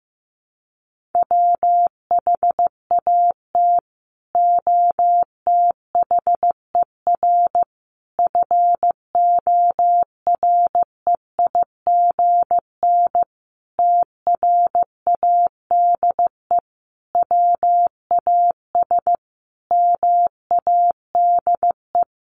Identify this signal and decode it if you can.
Morse code, heard as a keyed tone, about 15 wpm, 700 Hz. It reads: WHAT OTHER FOREIGN TRADE WAS MADE
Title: Sentences from Top 1400 Words 15wpm